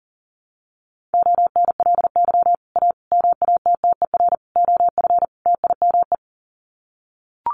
Morse code, heard as a keyed tone, about 40 wpm, 700 Hz. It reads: ONLY A MATTER OF TIME
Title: Sets of 5 Words Encore 40wpm